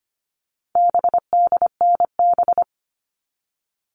Morse code, heard as a keyed tone, about 25 wpm, 700 Hz. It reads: BDNB
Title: All Cut Numbers - 4-Digit 25wpm